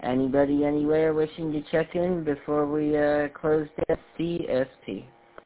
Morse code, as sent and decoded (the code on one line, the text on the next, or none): none